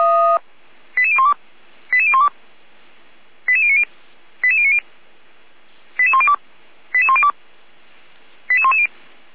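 Five-tone selective call (ZVEI), heard as quick runs of five tones, tones of about 70 ms. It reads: 88012 88012 88090 88090 88102 88102 88110
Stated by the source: SDR-received ZVEI five-tone selective call, address 88352.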